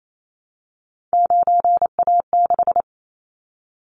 Morse code, heard as a keyed tone, about 28 wpm, 700 Hz. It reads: 9A6
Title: Callsign Prefixes for Top 100 DXCC - 28wpm